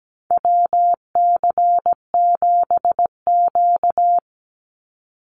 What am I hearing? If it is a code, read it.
Morse code, heard as a keyed tone, about 17 wpm, 700 Hz. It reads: WC7Q
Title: Call Signs Repeated 2x - 17wpm